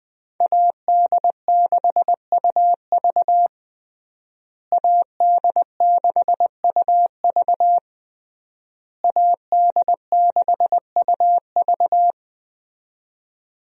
Morse code, heard as a keyed tone, about 20 wpm, 700 Hz. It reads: AD6UV AD6UV AD6UV
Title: USA Call Signs - 5 Characters - Repeated 3x - 20wpm